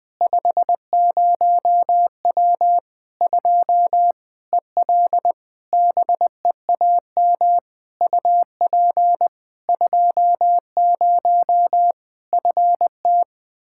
Morse code, heard as a keyed tone, about 20 wpm, 700 Hz. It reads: 50W 2 EL BEAM UP 20 FT
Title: Practice QSO - Exchange 2 - Info - 20wpm